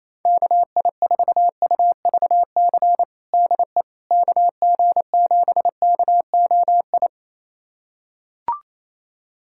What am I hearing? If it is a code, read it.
Morse code, heard as a keyed tone, about 28 wpm, 700 Hz. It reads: KI4UVC DE KG7KOI
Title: USA Callsign DE Callsign - 6 Characters - 28wpm